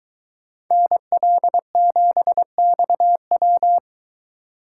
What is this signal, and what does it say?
Morse code, heard as a keyed tone, about 23 wpm, 700 Hz. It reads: NL7XW